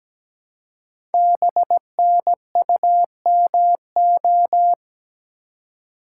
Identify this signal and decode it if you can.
Morse code, heard as a keyed tone, about 17 wpm, 700 Hz. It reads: BNUMO
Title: Code Groups - 5 Letters - 17wpm